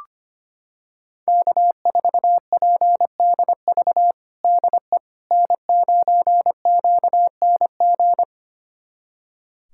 Morse code, heard as a keyed tone, about 25 wpm, 700 Hz. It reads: K4PDV DE N9QNG
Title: USA Callsign DE Callsign 25wpm